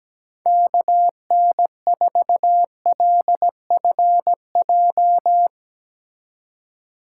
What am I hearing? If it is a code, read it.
Morse code, heard as a keyed tone, about 17 wpm, 700 Hz. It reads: KN4LFJ
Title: USA Call Signs - 6 Characters - 2x Character Spacing - 17wpm